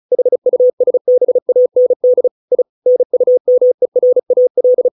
Morse code, heard as a keyed tone, about 34 wpm, 500 Hz. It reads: HUSBAND I NUMERAL
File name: 632 words / 632 Words 35WPM